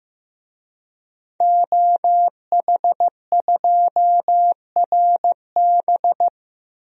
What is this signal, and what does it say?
Morse code, heard as a keyed tone, about 15 wpm, 700 Hz. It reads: OH2RB
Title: DXCC-100 Call Signs - 5 Characters - Repeated 2x - 15wpm